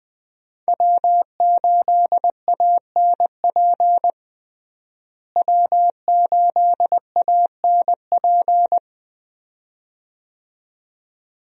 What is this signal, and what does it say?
Morse code, heard as a keyed tone, about 20 wpm, 700 Hz. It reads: W8ANP W8ANP
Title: USA Call Signs - 5 Characters - Repeated 2x - 20wpm